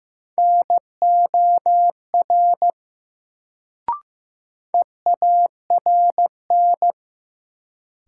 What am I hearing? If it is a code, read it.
Morse code, heard as a keyed tone, about 16 wpm, 700 Hz. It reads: NOR EARN
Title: Sentences from Top 1500 Words - New Word Drill Alt - 15wpm